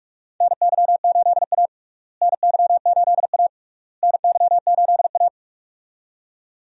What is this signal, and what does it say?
Morse code, heard as a keyed tone, about 44 wpm, 700 Hz. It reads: NY8A NY8A NY8A
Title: USA Call Signs - 4 Characters - Repeated 3x - 45wpm